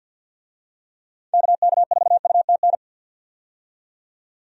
Morse code, heard as a keyed tone, about 51 wpm, 700 Hz. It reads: KK4UTN
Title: USA Call Signs - 6 Characters - 50wpm